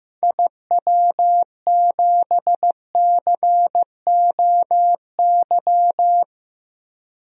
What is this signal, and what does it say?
Morse code, heard as a keyed tone, about 15 wpm, 700 Hz. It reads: IW7COY
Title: DXCC-100 Call Signs - 6 Characters - Repeated 2x - 15wpm